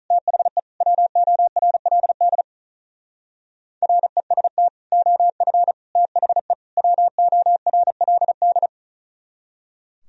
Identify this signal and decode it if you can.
Morse code, heard as a keyed tone, about 43 wpm, 700 Hz. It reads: THE WORLD REST OF THE WORLD
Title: Speed-Racing - Sets of 4 Words 35wpm